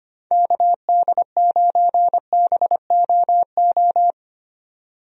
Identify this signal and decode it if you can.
Morse code, heard as a keyed tone, about 25 wpm, 700 Hz. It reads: KD9BOO
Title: USA Call Signs - 6 Characters - 25wpm